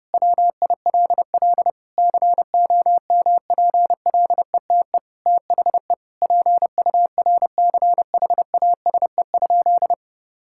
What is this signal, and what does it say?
Morse code, heard as a keyed tone, about 30 wpm, 700 Hz. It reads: WILL COMPLETE THE PURCHASE?
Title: Sentences from Top 1000 Words - Review Part 5 - 30wpm